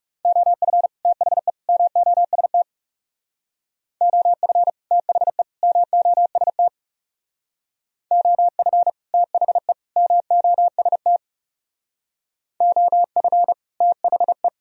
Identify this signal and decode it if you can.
Morse code, heard as a keyed tone, about 38 wpm, 700 Hz. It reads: OF THE MOST OF THE MOST OF THE MOST OF THE